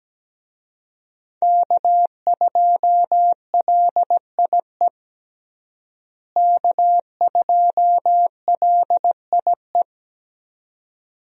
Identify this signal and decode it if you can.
Morse code, heard as a keyed tone, about 17 wpm, 700 Hz. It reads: K2LIE K2LIE
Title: USA Call Signs - 5 Characters - Repeated 2x - 17wpm